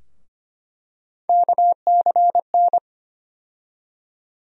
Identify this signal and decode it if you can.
Morse code, heard as a keyed tone, about 25 wpm, 700 Hz. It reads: KCN
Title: USA Call Sign Postfixes - 3 Letters 25wpm